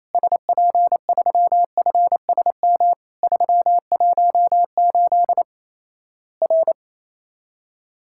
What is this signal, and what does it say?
Morse code, heard as a keyed tone, about 28 wpm, 700 Hz. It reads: SP3FSM 318 R
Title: Worked All Europe DX Contest - QTC Exchange - 28wpm